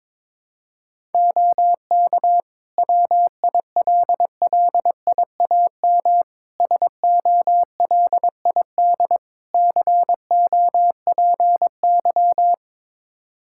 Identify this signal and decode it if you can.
Morse code, heard as a keyed tone, about 22 wpm, 700 Hz. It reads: OK WILLIAM SOLID COPY
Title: Practice QSO - Exchange 3 - More Info - 22wpm